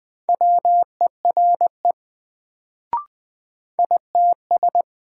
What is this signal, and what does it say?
Morse code, heard as a keyed tone, about 20 wpm, 700 Hz. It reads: WERE ITS
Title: Sets of 2 Words - New Word Drill - 20wpm